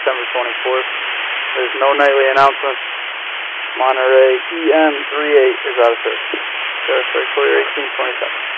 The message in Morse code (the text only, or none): none